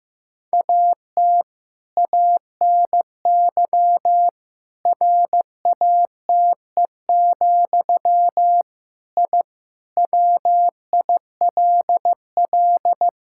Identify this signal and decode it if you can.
Morse code, heard as a keyed tone, about 15 wpm, 700 Hz. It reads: AT ANY RATE, I WILL
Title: Sentences from Top 1000 Words - Review Part 6 - 15wpm